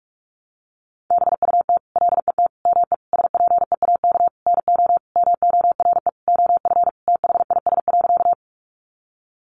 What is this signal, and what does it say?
Morse code, heard as a keyed tone, about 44 wpm, 700 Hz. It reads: BUT LET ME SPEAK NO MORE OF THIS.
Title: Sentences from Top 1100 Words 45wpm